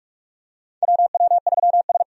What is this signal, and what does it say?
Morse code, heard as a keyed tone, about 46 wpm, 700 Hz. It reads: WW3S
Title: USA Call Signs - 4 Characters - 45wpm